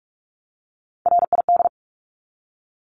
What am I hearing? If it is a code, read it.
Morse code, heard as a keyed tone, about 44 wpm, 700 Hz. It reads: RID
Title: Top 1751-2000 Words 45wpm